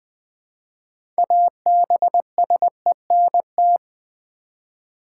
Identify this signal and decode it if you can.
Morse code, heard as a keyed tone, about 20 wpm, 700 Hz. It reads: ABSENT
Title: Build A Word - 2 Letters At A Time - 6-Letter Words - 20wpm